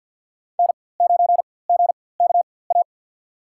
Morse code, heard as a keyed tone, about 53 wpm, spaced as for 24 wpm, 700 Hz. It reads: N9GKA
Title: USA Call Signs - 5 Characters - 4x Character Spacing - 50wpm